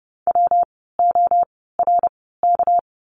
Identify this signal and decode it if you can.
Morse code, heard as a keyed tone, about 30 wpm, spaced as for 17 wpm, 700 Hz. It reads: WORK
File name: course / 30:10wpm / 237 Lesson 38f - Letter q - Words 30wpm